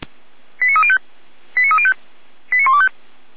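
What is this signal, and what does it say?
Five-tone selective call (ZVEI), heard as quick runs of five tones, tones of about 70 ms. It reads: 89305 89305 89125
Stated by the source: Five-tone ZVEI selective call, received on an SDR.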